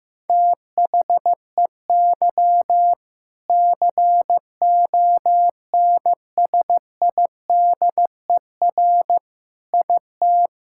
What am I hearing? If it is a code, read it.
Morse code, heard as a keyed tone, about 16 wpm, 700 Hz. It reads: THEY CONSIDER IT
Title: Sentences from Top 1000 Words - Review Part 5 - 15wpm